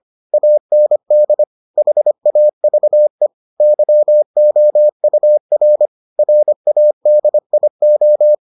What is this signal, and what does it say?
Morse code, heard as a keyed tone, about 25 wpm, 600 Hz. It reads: AND HAVE YOUR RADIO